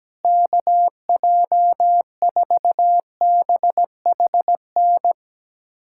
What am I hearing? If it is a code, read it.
Morse code, heard as a keyed tone, about 17 wpm, 700 Hz. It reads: KJ4BHN
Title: USA Callsign DE Callsign 17wpm